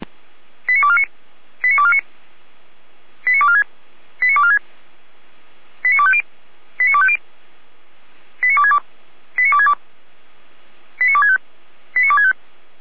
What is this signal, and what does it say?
Five-tone selective call (ZVEI), heard as quick runs of five tones, tones of about 70 ms. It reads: 89249 89249 89246 89246 89250 89250 89262 89262 89265 89265
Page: /455/SDR/ZVEI/